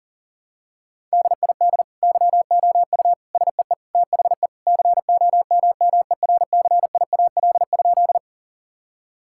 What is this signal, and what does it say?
Morse code, heard as a keyed tone, about 40 wpm, 700 Hz. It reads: DID YOU SEE THE COMMERCIAL?
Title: Sentences from Top 1000 Words - Review Part 5 - 40wpm